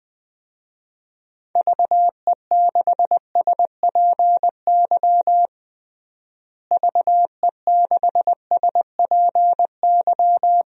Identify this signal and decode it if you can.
Morse code, heard as a keyed tone, about 20 wpm, 700 Hz. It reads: VE6SPY VE6SPY
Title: DXCC-100 Call Signs - 6 Characters - Repeated 2x - 20wpm